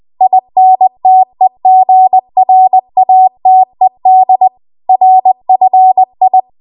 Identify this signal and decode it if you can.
Morse code, heard as a keyed tone, about 20 wpm, 750 Hz. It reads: INTEGRATED RFI
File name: Morse / 20WPM